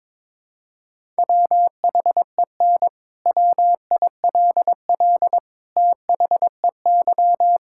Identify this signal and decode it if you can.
Morse code, heard as a keyed tone, about 22 wpm, 700 Hz. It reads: WHEN WILL THEY
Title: Sentences from Top 1600 Words 22wpm